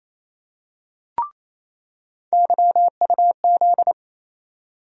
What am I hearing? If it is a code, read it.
Morse code, heard as a keyed tone, about 28 wpm, 700 Hz. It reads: YUZ
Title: USA Call Sign Postfixes - 3 Letters 28wpm